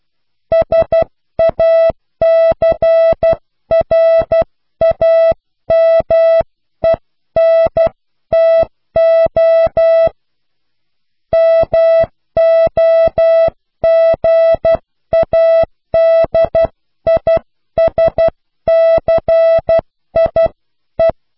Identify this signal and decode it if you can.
Morse code, heard as a keyed tone, about 12 wpm, 660 Hz. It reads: SACRAMENTO MOGADISCIE
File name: cw / pseudo worden(10)